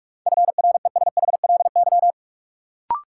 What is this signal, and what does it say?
Morse code, heard as a keyed tone, about 46 wpm, 700 Hz. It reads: FRESHLY
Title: Sets of 5 Words Encore - New Word Drill - 45wpm